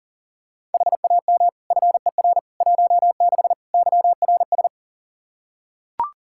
Morse code, heard as a keyed tone, about 40 wpm, 700 Hz. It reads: HAM FER 16 YRS
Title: K1USN SST QSO Practice 40wpm